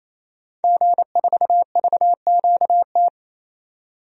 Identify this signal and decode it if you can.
Morse code, heard as a keyed tone, about 28 wpm, 700 Hz. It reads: G4VQT
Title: DXCC-100 Call Signs - 5 Characters - 28wpm